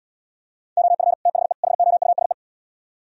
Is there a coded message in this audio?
Yes: Morse code from 0.8 to 2.3 s, heard as a keyed tone, about 79 wpm, 700 Hz. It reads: OF THE HOUSE